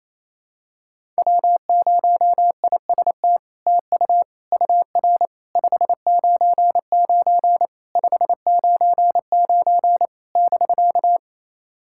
Morse code, heard as a keyed tone, about 28 wpm, 700 Hz. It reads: W0IST TU UR 599 599 <BK>
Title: SOTA QSO Practice 28wpm